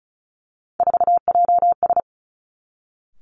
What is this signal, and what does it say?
Morse code, heard as a keyed tone, about 36 wpm, 700 Hz. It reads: 4JS
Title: Three Challenging Letter-Number Combinations 35wpm